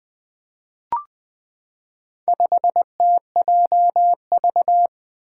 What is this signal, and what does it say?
Morse code, heard as a keyed tone, about 20 wpm, 700 Hz. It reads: 5TJV
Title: Four Letter-Number Combinations - 20wpm